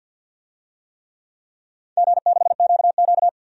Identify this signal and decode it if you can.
Morse code, heard as a keyed tone, about 52 wpm, 700 Hz. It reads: G6QQ
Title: DXCC-100 Call Signs - 4 Characters - 50wpm